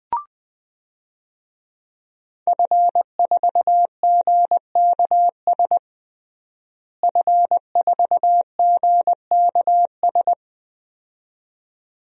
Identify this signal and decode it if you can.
Morse code, heard as a keyed tone, about 20 wpm, 700 Hz. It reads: F4GKS F4GKS
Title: DXCC-100 Call Signs - 5 Characters - Repeated 2x - 20wpm